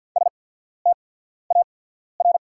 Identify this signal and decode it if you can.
Morse code, heard as a keyed tone, about 48 wpm, spaced as for 14 wpm, 700 Hz. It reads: STAR
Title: 8x Character Spacing - Top 500 Words - 50wpm